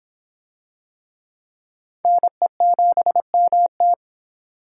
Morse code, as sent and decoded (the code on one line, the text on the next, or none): -. --... -- -
NE7MT